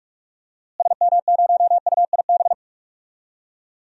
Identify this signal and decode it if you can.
Morse code, heard as a keyed tone, about 42 wpm, 700 Hz. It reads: SM0UIB
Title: DXCC-100 Call Signs - 6 Characters - Repeated 3x - 45wpm